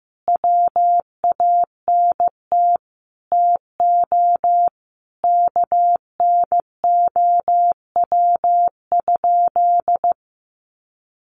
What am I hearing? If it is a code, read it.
Morse code, heard as a keyed tone, about 15 wpm, 700 Hz. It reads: WANT TO KNOW?
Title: Sentences from Top 200 Words 15wpm